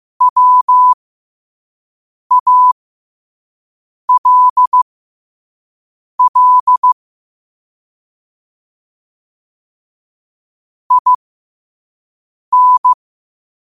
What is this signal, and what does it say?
Morse code, heard as a keyed tone, about 15 wpm, spaced as for 5 wpm, 1000 Hz. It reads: WALL IN
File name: qotdfiles / 5wpm